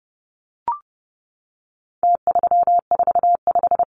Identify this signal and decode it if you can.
Morse code, heard as a keyed tone, about 30 wpm, 700 Hz. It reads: T345
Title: Numbers - Contest Style - 4-Digit 30wpm